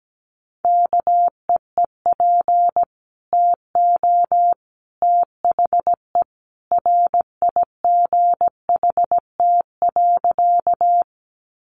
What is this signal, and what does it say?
Morse code, heard as a keyed tone, about 17 wpm, 700 Hz. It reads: KEEP TO THE RIGHT.
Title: Sentences from Top 300 Words 17wpm